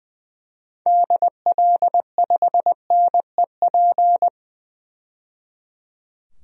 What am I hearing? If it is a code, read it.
Morse code, heard as a keyed tone, about 20 wpm, 700 Hz. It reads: DL5NEP
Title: DXCC-100 Call Signs - 6 Characters - 20wpm